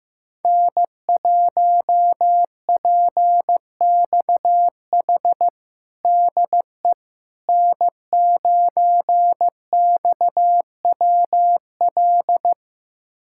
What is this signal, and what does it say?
Morse code, heard as a keyed tone, about 15 wpm, 700 Hz. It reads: N1PXH DE N9XWL
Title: USA Callsign DE Callsign - 5 Characters - 15wpm